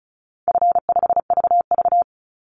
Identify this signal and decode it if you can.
Morse code, heard as a keyed tone, about 34 wpm, 700 Hz. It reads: F5VV
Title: Easy Call Signs Repeated 3x - 35wpm